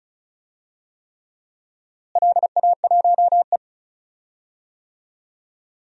Morse code, heard as a keyed tone, about 36 wpm, 700 Hz. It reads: LA1E
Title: DXCC-100 Call Signs - 4 Characters - Repeated 3x - 35wpm